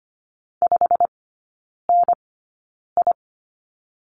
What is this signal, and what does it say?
Morse code, heard as a keyed tone, about 25 wpm, spaced as for 9 wpm, 690 Hz. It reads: 5NI
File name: Lesson 15e - Number 5 - Three Characters 25wpm